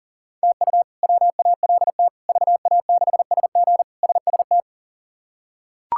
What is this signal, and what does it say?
Morse code, heard as a keyed tone, about 37 wpm, 700 Hz. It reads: TU WALT VA6SG SST
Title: K1USN SST QSO Practice - 40wpm